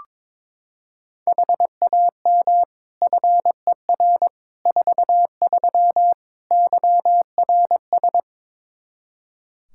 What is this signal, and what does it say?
Morse code, heard as a keyed tone, about 21 wpm, 700 Hz. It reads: HAM FER 43 YRS
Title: K1USN SST QSO Practice 22wpm